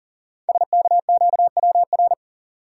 Morse code, heard as a keyed tone, about 40 wpm, 700 Hz. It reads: SKQWR
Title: Code Groups - 5 Letter:Numbers - 40wpm